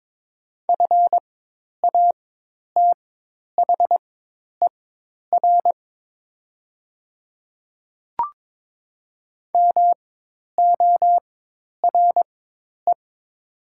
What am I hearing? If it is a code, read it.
Morse code, heard as a keyed tone, about 22 wpm, spaced as for 10 wpm, 700 Hz. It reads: FATHER MORE
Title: 4x Character Spacing - Top 200 Words - 22wpm